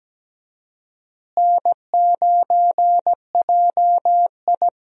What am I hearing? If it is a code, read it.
Morse code, heard as a keyed tone, about 17 wpm, 700 Hz. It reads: N9JI
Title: USA Call Signs - 4 Characters - Repeated 3x - 17wpm